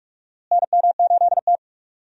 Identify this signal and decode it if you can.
Morse code, heard as a keyed tone, about 42 wpm, 700 Hz. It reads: NM8T
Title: USA Call Signs - 4 Characters - Repeated 3x - 45wpm